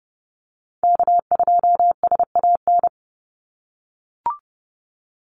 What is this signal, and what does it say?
Morse code, heard as a keyed tone, about 30 wpm, 700 Hz. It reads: K2SAN
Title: Call Signs - 30wpm